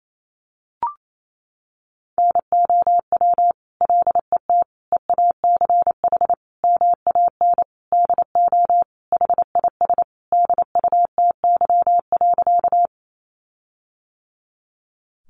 Morse code, heard as a keyed tone, about 28 wpm, 700 Hz. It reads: NOW LET EACH MAN DO HIS DUTY.
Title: Sentences from Top 1500 Words 28wpm